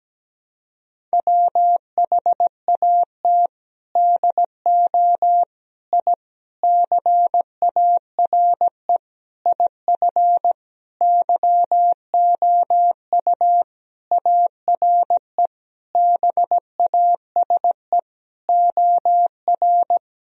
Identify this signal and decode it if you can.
Morse code, heard as a keyed tone, about 17 wpm, 700 Hz. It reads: WHAT DO I CARE IF YOU ARE BASE OR
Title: Sentences from Top 1000 Words - Review Part 1 - 17wpm